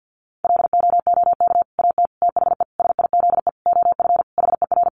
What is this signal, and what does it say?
Morse code, heard as a keyed tone, about 50 wpm, 700 Hz. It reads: LOOK AT THE SIZE OF HER
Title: Sentences from Top 1200 Words 50wpm